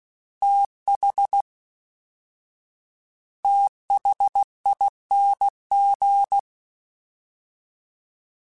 Morse code, heard as a keyed tone, about 16 wpm, 800 Hz. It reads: TH THING
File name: TH / TH mp3s / TH Prefixes 1x1 @13